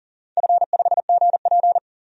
Morse code, heard as a keyed tone, about 40 wpm, 700 Hz. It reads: F5GP
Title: DXCC-100 Call Signs - 4 Characters - 40wpm